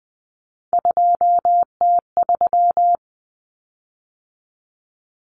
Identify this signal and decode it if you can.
Morse code, heard as a keyed tone, about 20 wpm, 700 Hz. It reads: 2T3
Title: Common Cut Numbers - 3-Digit 20wpm